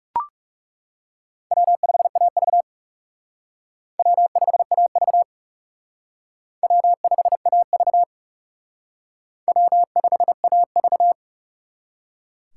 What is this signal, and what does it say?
Morse code, heard as a keyed tone, about 38 wpm, 700 Hz. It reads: W5AV W5AV W5AV W5AV